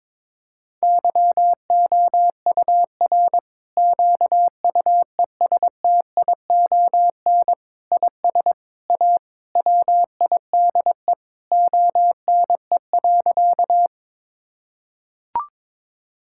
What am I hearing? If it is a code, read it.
Morse code, heard as a keyed tone, about 22 wpm, 700 Hz. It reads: YOUR QUESTION IS A WIDE ONE.
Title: Sentences from Top 1000 Words - Review Part 6 - 22wpm